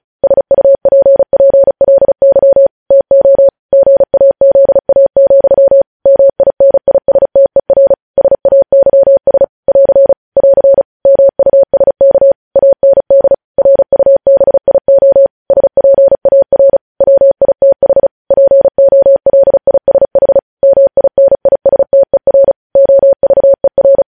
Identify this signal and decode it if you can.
Morse code, heard as a keyed tone, about 36 wpm, 550 Hz. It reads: SUPPLY TO GAZA, MINISTER SAYS <AR> <AR> MUSK AND RUBIO SPAR WITH POLISH MINISTER OVER
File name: podcast / News headlines-1x-35x35